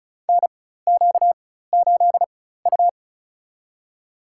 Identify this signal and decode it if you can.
Morse code, heard as a keyed tone, about 36 wpm, spaced as for 16 wpm, 700 Hz. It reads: NQ8U
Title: USA Call Signs - 4x Character Spacing - 35wpm